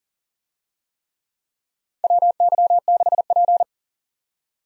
Morse code, heard as a keyed tone, about 40 wpm, 700 Hz. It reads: WY6P
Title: USA Call Signs - 4 Characters - 40wpm